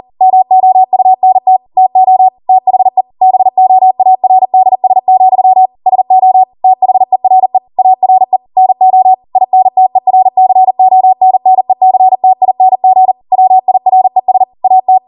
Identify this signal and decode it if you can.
Morse code, heard as a keyed tone, about 40 wpm, 750 Hz. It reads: MOUNT TO THE BOARDS, SO THERE ARE NO INTERCONNECTING WIRES AT
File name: Morse / 40WPM